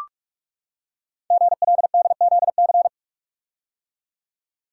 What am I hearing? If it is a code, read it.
Morse code, heard as a keyed tone, about 44 wpm, 700 Hz. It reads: GLDZC
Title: Code Groups - 5 Letters - 45wpm